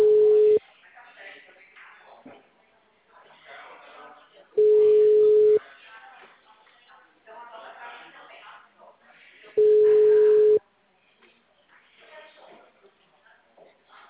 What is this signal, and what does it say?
Ringback tone (Europe, 425 Hz), with speech.
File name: out- -2010- - - .20308